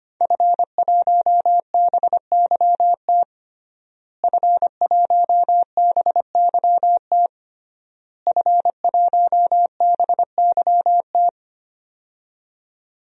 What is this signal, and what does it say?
Morse code, heard as a keyed tone, about 25 wpm, 700 Hz. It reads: F1BYT F1BYT F1BYT
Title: DXCC-100 Call Signs - 5 Characters - Repeated 3x - 25wpm